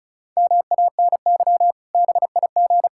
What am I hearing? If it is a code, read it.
Morse code, heard as a keyed tone, about 36 wpm, 700 Hz. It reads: MANY BIG
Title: Pangram Challenge v2 - 35wpm